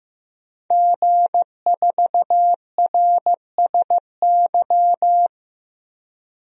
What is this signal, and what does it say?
Morse code, heard as a keyed tone, about 15 wpm, 700 Hz. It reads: G4RSY
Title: DXCC-100 Call Signs - 5 Characters - 15wpm